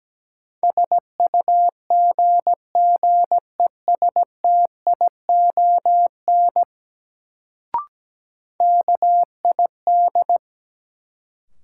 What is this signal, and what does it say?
Morse code, heard as a keyed tone, about 17 wpm, 700 Hz. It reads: SUGGESTION KID
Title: Sentences from Top 1600 Words - New Word Drill - 17wpm